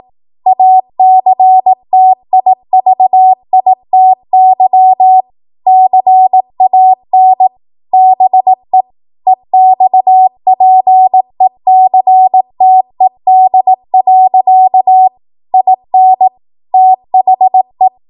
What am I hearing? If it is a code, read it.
Morse code, heard as a keyed tone, about 18 wpm, 750 Hz. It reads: ACTIVITY CAN BE EXPECTED. IN THE
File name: Morse / 18WPM